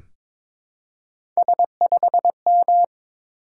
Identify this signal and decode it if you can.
Morse code, heard as a keyed tone, about 22 wpm, 700 Hz. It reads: S5M
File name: practice / 22wpm / Three Letter-Number Combinations - 22wpm